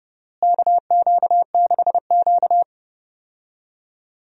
Morse code, heard as a keyed tone, about 30 wpm, 700 Hz. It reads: KQ6Q
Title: USA Call Signs - 4 Characters - Repeated 3x - 30wpm